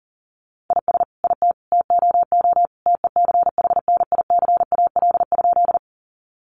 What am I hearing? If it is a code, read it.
Morse code, heard as a keyed tone, about 40 wpm, 700 Hz. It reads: IS IT TOO TECHNICAL?
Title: Sentences from Top 1100 Words 40wpm